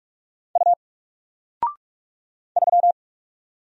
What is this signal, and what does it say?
Morse code, heard as a keyed tone, about 45 wpm, 700 Hz. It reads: U 3
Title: Single Letter-Number 45wpm